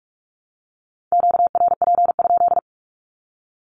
Morse code, heard as a keyed tone, about 44 wpm, 700 Hz. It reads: QRP?
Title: Q-Codes 45wpm